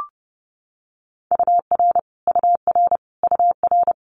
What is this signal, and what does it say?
Morse code, heard as a keyed tone, about 30 wpm, 700 Hz. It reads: UR UR UR
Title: Abbreviations Repeated 3x - 30wpm